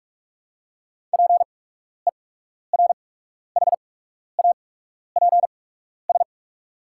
Morse code, heard as a keyed tone, about 48 wpm, spaced as for 12 wpm, 690 Hz. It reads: PERHAPS
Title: 8x Character Spacing - Top 500 Words - 45wpm